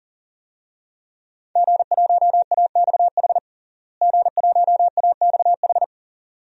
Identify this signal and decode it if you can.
Morse code, heard as a keyed tone, about 40 wpm, 700 Hz. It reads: G1AXH G1AXH
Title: DXCC-100 Call Signs - 5 Characters - Repeated 2x - 40wpm